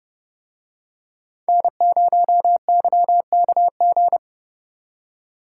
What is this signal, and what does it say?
Morse code, heard as a keyed tone, about 29 wpm, 700 Hz. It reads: N0YKG